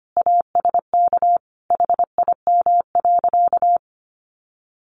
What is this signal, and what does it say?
Morse code, heard as a keyed tone, about 25 wpm, 700 Hz. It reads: ASK HIM.
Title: Sentences from Top 300 Words 25wpm